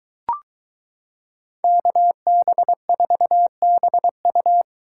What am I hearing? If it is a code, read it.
Morse code, heard as a keyed tone, about 22 wpm, 700 Hz. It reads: KB4BU